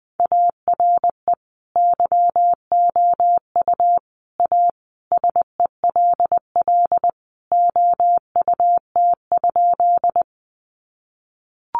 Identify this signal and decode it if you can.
Morse code, heard as a keyed tone, about 20 wpm, 700 Hz. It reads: ARE YOU A SELL OUT?
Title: Sentences from Top 1100 Words 20wpm